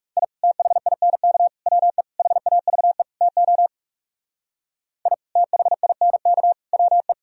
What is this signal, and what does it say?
Morse code, heard as a keyed tone, about 38 wpm, 700 Hz. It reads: I THINK WE HAVE TO I THINK WE